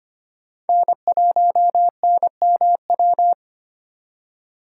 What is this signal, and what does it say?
Morse code, heard as a keyed tone, about 32 wpm, 700 Hz. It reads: N1NMW